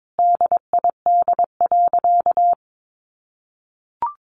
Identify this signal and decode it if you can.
Morse code, heard as a keyed tone, about 22 wpm, 700 Hz. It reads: DID.
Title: Sentences from Top 100 Words 22wpm